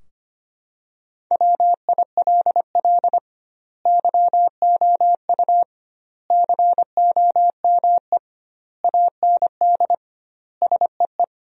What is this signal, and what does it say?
Morse code, heard as a keyed tone, about 25 wpm, 700 Hz. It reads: WILL YOU COME AND SEE
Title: 2x Word-Spacing - Sentences from Top 100 Words - 25wpm